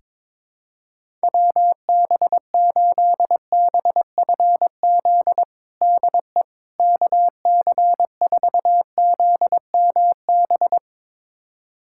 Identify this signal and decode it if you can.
Morse code, heard as a keyed tone, about 22 wpm, 700 Hz. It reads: WB8BFZ DE KC4ZMB
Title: USA Callsign DE Callsign - 6 Characters - 22wpm